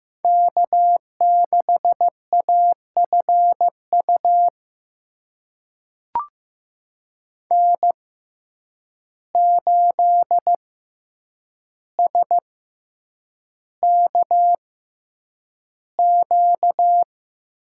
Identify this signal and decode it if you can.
Morse code, heard as a keyed tone, about 15 wpm, spaced as for 8 wpm, 700 Hz. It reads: K6AFU N8SKQ
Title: USA Call Signs - 5 Characters - 6x Character Spacing - 15wpm